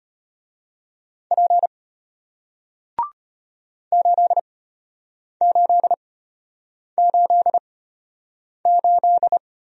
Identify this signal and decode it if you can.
Morse code, heard as a keyed tone, about 33 wpm, spaced as for 8 wpm, 700 Hz. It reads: P 8888